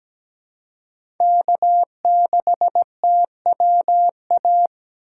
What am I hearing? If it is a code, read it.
Morse code, heard as a keyed tone, about 17 wpm, 700 Hz. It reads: K6TWA
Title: USA Call Signs - 2x Character Spacing - 17wpm